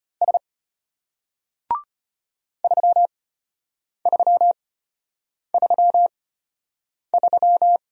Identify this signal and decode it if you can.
Morse code, heard as a keyed tone, about 33 wpm, spaced as for 8 wpm, 700 Hz. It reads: S 3333